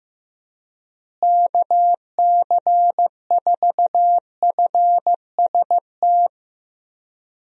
Morse code, heard as a keyed tone, about 15 wpm, 700 Hz. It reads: KC4FST
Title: USA Call Signs - 6 Characters - Repeated 2x - 15wpm